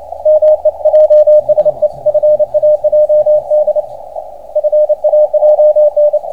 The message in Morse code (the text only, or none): MEJSE FAJD FAJN